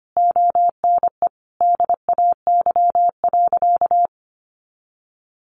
Morse code, heard as a keyed tone, about 25 wpm, 700 Hz. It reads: ONE DAY.
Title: Sentences from Top 500 Words 25wpm